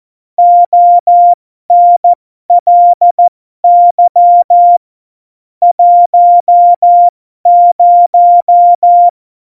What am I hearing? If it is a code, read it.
Morse code, heard as a keyed tone, about 14 wpm, spaced as for 12 wpm, 700 Hz. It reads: ONLY 10
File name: Morsekurs / Witze